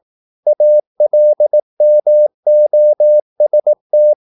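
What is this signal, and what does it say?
Morse code, heard as a keyed tone, about 18 wpm, 600 Hz. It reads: ALMOST
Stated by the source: Morse practice at 18 wpm, English text, from CW med Gnister.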